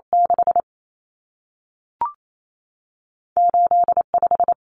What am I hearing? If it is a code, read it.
Morse code, heard as a keyed tone, about 28 wpm, 700 Hz. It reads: 6 8 5
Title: Common Cut Numbers - 2-Digit 28wpm